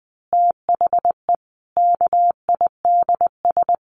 Morse code, heard as a keyed tone, about 20 wpm, 700 Hz. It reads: THE KIDS
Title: Easy Phrases 20wpm